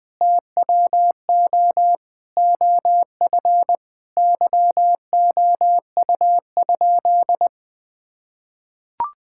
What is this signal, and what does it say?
Morse code, heard as a keyed tone, about 21 wpm, 700 Hz. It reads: TWO OF YOU?
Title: Sentences from Top 1000 Words - Review Part 6 - 20wpm